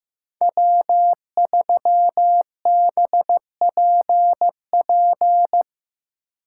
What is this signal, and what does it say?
Morse code, heard as a keyed tone, about 16 wpm, 700 Hz. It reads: W3BPP
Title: Speed Racing - USA Call Signs 15wpm